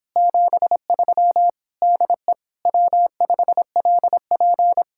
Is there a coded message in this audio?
Yes: Morse code from 0.2 to 4.8 s, heard as a keyed tone, about 26 wpm, 700 Hz. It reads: 73 DE W5LP